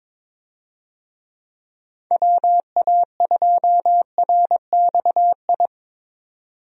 Morse code, heard as a keyed tone, about 23 wpm, 700 Hz. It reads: WA2RXI
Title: USA Call Signs - 6 Characters - 22wpm